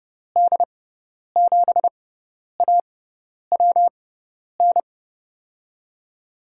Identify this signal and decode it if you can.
Morse code, heard as a keyed tone, about 32 wpm, spaced as for 10 wpm, 700 Hz. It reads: D7AWN
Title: USA Call Signs - 6 Characters - 6x Character Spacing - 30wpm